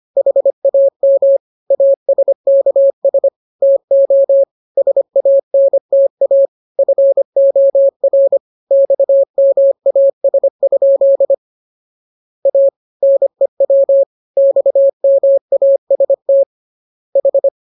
Morse code, heard as a keyed tone, about 25 wpm, 550 Hz. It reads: HAM ASKS TO SANTA FOR XMAS? A NEW XMAST H
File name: jokes / joke 16 25 0000